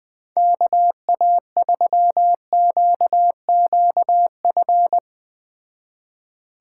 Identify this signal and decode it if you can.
Morse code, heard as a keyed tone, about 19 wpm, 700 Hz. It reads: KA3QQF
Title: Speed Racing - USA 2x3 Call Signs 17wpm